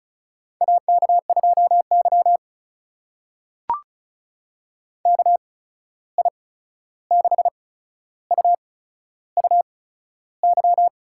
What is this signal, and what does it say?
Morse code, heard as a keyed tone, about 36 wpm, spaced as for 13 wpm, 700 Hz. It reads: AK2Y KI6UUY
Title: USA Call Signs - 8x Character Spacing - 35wpm